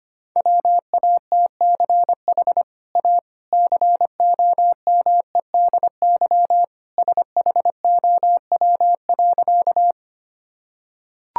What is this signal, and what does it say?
Morse code, heard as a keyed tone, about 25 wpm, 700 Hz. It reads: WATCH A COMEDY SHOW.
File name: Easy Sentences with Hints - 25wpm